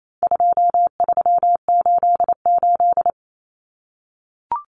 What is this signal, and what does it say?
Morse code, heard as a keyed tone, about 28 wpm, 700 Hz. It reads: 2388
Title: Numbers - 4-Digit 28wpm